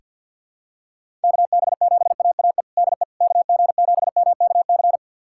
Morse code, heard as a keyed tone, about 49 wpm, 700 Hz. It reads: KB7AAE DE KG7MKC
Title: USA Callsign DE Callsign - 6 Characters - 50wpm